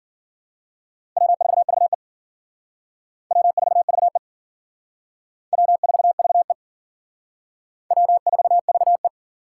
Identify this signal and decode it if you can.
Morse code, heard as a keyed tone, about 50 wpm, 700 Hz. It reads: W4VE W4VE W4VE W4VE